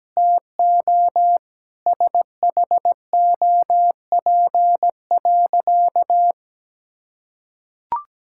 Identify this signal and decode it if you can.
Morse code, heard as a keyed tone, about 17 wpm, 700 Hz. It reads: TO SHOP.
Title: Sentences from Top 1600 Words 17wpm